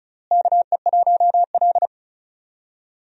Morse code, heard as a keyed tone, about 36 wpm, 700 Hz. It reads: KE1L
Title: USA Call Signs - 4 Characters - Repeated 2x - 35wpm